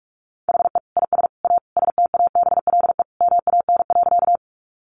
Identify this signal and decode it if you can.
Morse code, heard as a keyed tone, about 44 wpm, 700 Hz. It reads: HE IS A STABLE MAN.
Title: Sentences from Top 1500 Words 45wpm